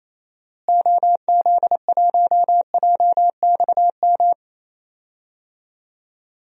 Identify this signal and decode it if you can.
Morse code, heard as a keyed tone, about 28 wpm, 700 Hz. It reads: OZ1JXM
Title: DXCC-100 Call Signs - 6 Characters - Repeated 2x - 28wpm